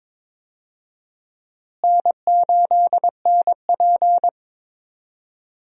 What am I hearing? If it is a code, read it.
Morse code, heard as a keyed tone, about 22 wpm, 700 Hz. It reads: N8NP
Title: USA Call Signs - 4 Characters - Repeated 2x - 22wpm